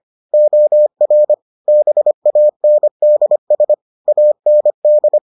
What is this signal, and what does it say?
Morse code, heard as a keyed tone, about 25 wpm, 600 Hz. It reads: OR BANDS AND